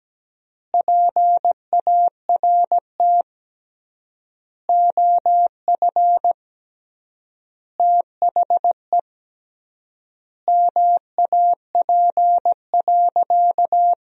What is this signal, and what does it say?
Morse code, heard as a keyed tone, about 17 wpm, 700 Hz. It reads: PART OF THE MAP.
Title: 3x Word-Spacing - Sentences from Top 400 Words - 17wpm